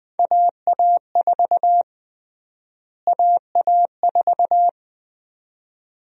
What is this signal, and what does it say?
Morse code, heard as a keyed tone, about 21 wpm, 700 Hz. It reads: AA4 AA4
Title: USA Call Sign Prefixes - Repeated 2x - 20wpm